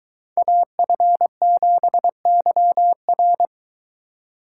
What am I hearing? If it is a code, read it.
Morse code, heard as a keyed tone, about 23 wpm, 700 Hz. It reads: AF7YR